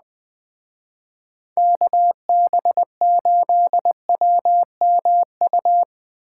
Morse code, heard as a keyed tone, about 20 wpm, 700 Hz. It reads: KB8WMU